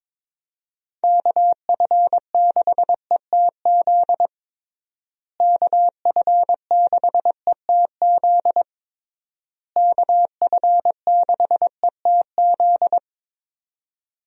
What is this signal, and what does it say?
Morse code, heard as a keyed tone, about 22 wpm, 700 Hz. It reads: KF6ETZ KF6ETZ KF6ETZ
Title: USA Call Signs - 6 Characters - Repeated 3x - 22wpm